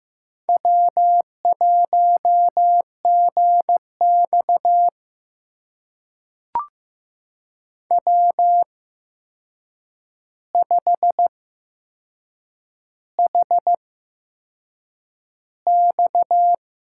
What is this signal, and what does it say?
Morse code, heard as a keyed tone, about 15 wpm, spaced as for 6 wpm, 700 Hz. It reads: W1GX W5HX
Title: USA Call Signs - 8x Character Spacing - 15wpm